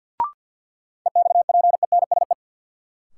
Morse code, heard as a keyed tone, about 50 wpm, 700 Hz. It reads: EXPENSE
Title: Sets of 4 Words - New Word Drill - 50wpm